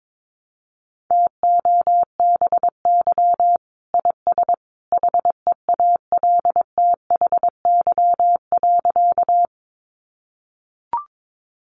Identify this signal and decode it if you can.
Morse code, heard as a keyed tone, about 22 wpm, 700 Hz. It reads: TOBY IS HEALTHY.
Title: Easy Phrases 22wpm